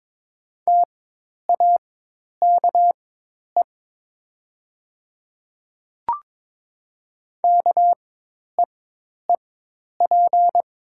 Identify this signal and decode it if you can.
Morse code, heard as a keyed tone, about 22 wpm, spaced as for 10 wpm, 700 Hz. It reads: TAKE KEEP
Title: 4x Character Spacing - Top 500 Words - 22wpm